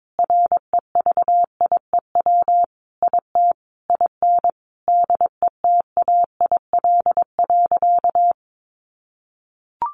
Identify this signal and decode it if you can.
Morse code, heard as a keyed tone, about 22 wpm, 700 Hz. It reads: REVIEW IT IN DETAIL.
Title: Sentences from Top 1500 Words 22wpm